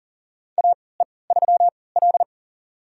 Morse code, heard as a keyed tone, about 40 wpm, spaced as for 23 wpm, 700 Hz. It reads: AE3L
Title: USA Call Signs - 4 Characters - 3x Character Spacing - 40wpm